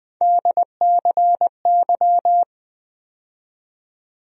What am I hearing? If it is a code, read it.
Morse code, heard as a keyed tone, about 20 wpm, 700 Hz. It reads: DCY
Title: USA Call Sign Postfixes - 3 Letters 20wpm